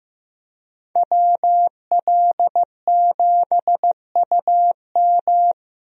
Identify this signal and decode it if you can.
Morse code, heard as a keyed tone, about 16 wpm, 700 Hz. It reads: WL7UM
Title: USA Call Signs - 5 Characters - Repeated 3x - 15wpm